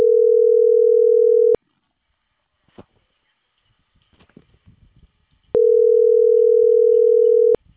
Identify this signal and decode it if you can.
Ringback tone (North America), 440 + 480 Hz.